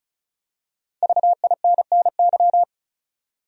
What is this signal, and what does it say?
Morse code, heard as a keyed tone, about 34 wpm, 700 Hz. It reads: VINNY
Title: USA Common Names - 35wpm